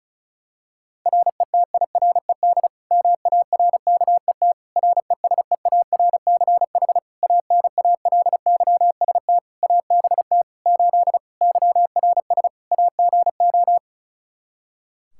Morse code, heard as a keyed tone, about 36 wpm, 700 Hz. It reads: RETIRED MARKET RESEARCH ANALYST ABT 8 YRS AGO
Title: Practice QSO - Exchange 3 - More Info - 35wpm